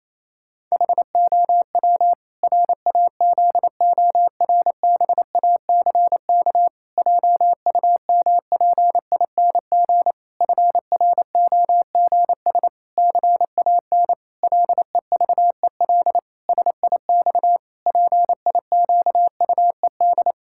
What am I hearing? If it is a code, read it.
Morse code, heard as a keyed tone, about 28 wpm, 700 Hz. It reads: HOW RAZORBACK JUMPING FROGS CAN LEVEL SIX PIQUED
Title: Pangram Challenge v2 - 28wpm